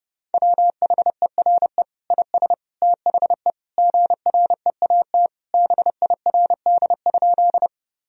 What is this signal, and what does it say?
Morse code, heard as a keyed tone, about 30 wpm, 700 Hz. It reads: WHERE IS THE GREAT BIRD?
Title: Sentences from Top 1000 Words - Review Part 2 - 30wpm